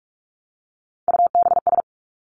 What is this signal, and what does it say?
Morse code, heard as a keyed tone, about 46 wpm, 700 Hz. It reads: UBS
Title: Three Challenging Letter-Number Combinations 45wpm